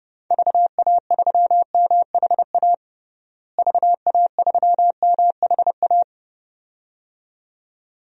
Morse code, heard as a keyed tone, about 30 wpm, 700 Hz. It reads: VA3MHA VA3MHA
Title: DXCC-100 Call Signs - 6 Characters - Repeated 2x - 30wpm